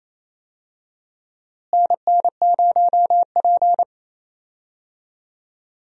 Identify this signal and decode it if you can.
Morse code, heard as a keyed tone, about 28 wpm, 700 Hz. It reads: NN0P
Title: USA Call Signs - 4 Characters - Repeated 3x - 28wpm